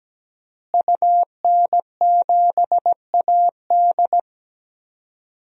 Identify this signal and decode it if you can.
Morse code, heard as a keyed tone, about 17 wpm, 700 Hz. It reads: UN7AD
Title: DXCC-100 Call Signs - 5 Characters - Repeated 3x - 17wpm